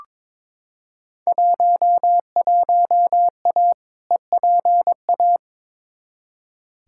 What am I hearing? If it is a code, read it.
Morse code, heard as a keyed tone, about 21 wpm, 700 Hz. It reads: 11A EPA
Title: Field Day Exchanges Encore - 22wpm